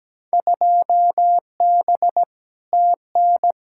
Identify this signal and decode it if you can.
Morse code, heard as a keyed tone, about 17 wpm, 700 Hz. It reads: 2B TN
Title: Field Day Exchanges Encore - 17wpm